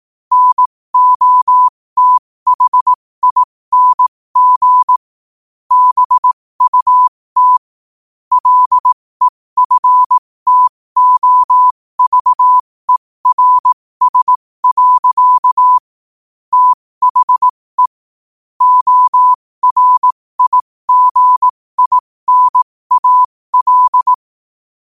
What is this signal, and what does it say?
Morse code, heard as a keyed tone, about 18 wpm, spaced as for 15 wpm, 1000 Hz. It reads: NOTHING BUT LEFTOVERS. THE ORIGINAL
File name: qotdfiles / 15wpm